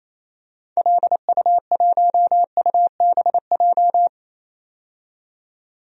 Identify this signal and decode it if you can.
Morse code, heard as a keyed tone, about 28 wpm, 700 Hz. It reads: LU1UBJ
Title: DXCC-100 Call Signs - 6 Characters - 28wpm